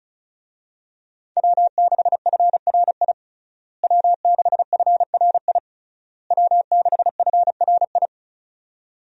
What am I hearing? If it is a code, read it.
Morse code, heard as a keyed tone, about 36 wpm, 700 Hz. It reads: W6FRI W6FRI W6FRI
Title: USA Call Signs - 5 Characters - Repeated 3x - 35wpm